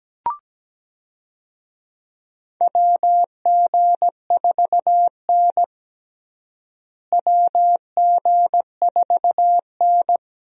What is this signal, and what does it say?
Morse code, heard as a keyed tone, about 17 wpm, 700 Hz. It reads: WG4N WG4N
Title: USA Call Signs - 4 Characters - Repeated 3x - 17wpm